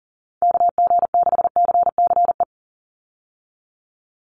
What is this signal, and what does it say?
Morse code, heard as a keyed tone, about 40 wpm, 700 Hz. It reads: KG6CCE
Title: USA Call Signs 40wpm